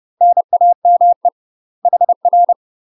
Morse code, heard as a keyed tone, about 30 wpm, 700 Hz. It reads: NAME HR
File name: UK MASTER QSOs TRAINING / 30WPM / UK MASTER QSOs 0034